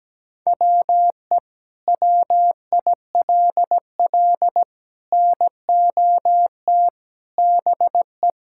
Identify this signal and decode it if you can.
Morse code, heard as a keyed tone, about 17 wpm, 700 Hz. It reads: WE WILL NOT BE
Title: Sentences from Top 1000 Words - Review Part 5 - 17wpm